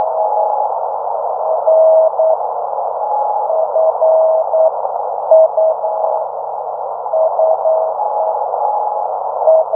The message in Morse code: none